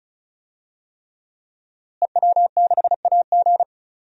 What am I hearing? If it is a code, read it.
Morse code, heard as a keyed tone, about 36 wpm, 700 Hz. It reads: EW6AG
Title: DXCC-100 Call Signs - 5 Characters - Repeated 3x - 35wpm